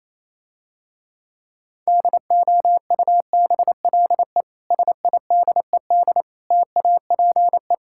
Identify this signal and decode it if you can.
Morse code, heard as a keyed tone, about 27 wpm, 700 Hz. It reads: DOUBLE SIDED TAPE
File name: Easy Words - Speed-Racing with Hints - 22wpm